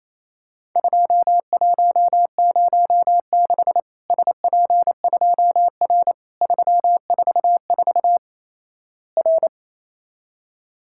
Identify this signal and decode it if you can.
Morse code, heard as a keyed tone, about 28 wpm, 700 Hz. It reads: 2106 SP2R 344 R
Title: Worked All Europe DX Contest - QTC Exchange - 28wpm